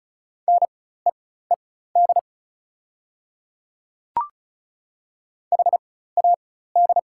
Morse code, heard as a keyed tone, about 36 wpm, spaced as for 16 wpm, 690 Hz. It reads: NEED HAD
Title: 4x Character Spacing - Top 200 Words - 35wpm